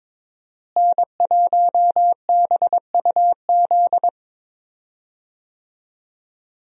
Morse code, heard as a keyed tone, about 22 wpm, 700 Hz. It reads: N1BUZ
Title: USA Call Signs - 5 Characters - Repeated 2x - 22wpm